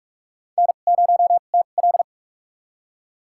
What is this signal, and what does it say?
Morse code, heard as a keyed tone, about 46 wpm, spaced as for 33 wpm, 700 Hz. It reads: N0TL
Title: USA Call Signs - 4 Characters - 2x Character Spacing - 45wpm